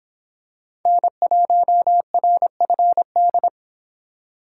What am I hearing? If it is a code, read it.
Morse code, heard as a keyed tone, about 25 wpm, 700 Hz. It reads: N1RFD